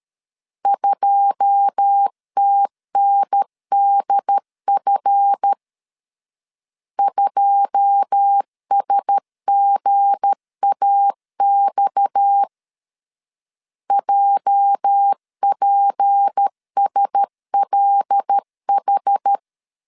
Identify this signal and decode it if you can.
Morse code, heard as a keyed tone, about 13 wpm, 800 Hz. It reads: 2TNDF 2SGAX JPSLH
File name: LEZIONI / 9 2 12wpm 2301